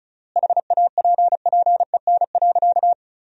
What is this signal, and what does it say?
Morse code, heard as a keyed tone, about 36 wpm, 700 Hz. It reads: HAPPEN.
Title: 3x Word-Spacing - Sentences from Top 400 Words - 35wpm